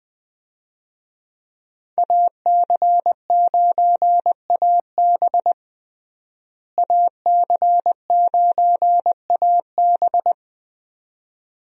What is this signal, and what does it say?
Morse code, heard as a keyed tone, about 20 wpm, 700 Hz. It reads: AC9AB AC9AB
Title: USA Call Signs - 5 Characters - Repeated 2x - 20wpm